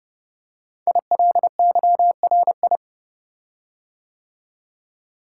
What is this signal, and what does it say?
Morse code, heard as a keyed tone, about 30 wpm, 700 Hz. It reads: ILYRI
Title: Code Groups - 5 Letters - 30wpm